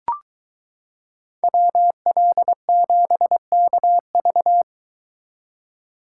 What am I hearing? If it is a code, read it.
Morse code, heard as a keyed tone, about 23 wpm, 700 Hz. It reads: WL7KV